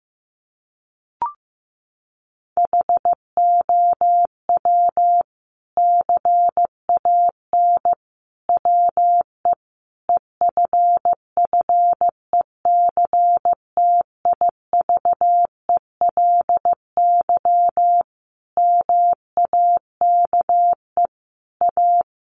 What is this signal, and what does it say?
Morse code, heard as a keyed tone, about 15 wpm, 700 Hz. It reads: HOW CAN WE EFFECTIVELY MAKE A
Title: Sentences from Top 1400 Words 15wpm